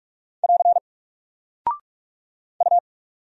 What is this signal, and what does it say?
Morse code, heard as a keyed tone, about 44 wpm, 700 Hz. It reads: <AR> U
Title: Single Character (Letter:Number:Punctuation:Prosign) 45wpm